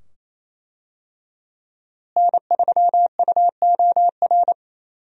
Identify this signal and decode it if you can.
Morse code, heard as a keyed tone, about 28 wpm, 700 Hz. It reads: N3UOR
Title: USA Call Signs - 5 Characters - Repeated 2x - 28wpm